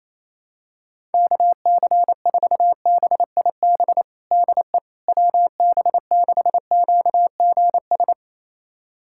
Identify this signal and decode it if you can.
Morse code, heard as a keyed tone, about 28 wpm, 700 Hz. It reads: KC4BIB DE WB6QGS
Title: USA Callsign DE Callsign - 6 Characters - 28wpm